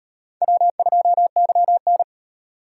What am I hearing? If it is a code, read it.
Morse code, heard as a keyed tone, about 38 wpm, 700 Hz. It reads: W2YN